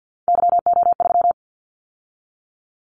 Morse code, heard as a keyed tone, about 50 wpm, 700 Hz. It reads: YO3
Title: Callsign Prefixes for Top 25 DXCC - 50wpm